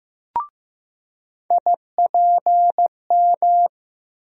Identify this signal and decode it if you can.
Morse code, heard as a keyed tone, about 15 wpm, 700 Hz. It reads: IPM
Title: USA Call Sign Postfixes - 3 Letters 15wpm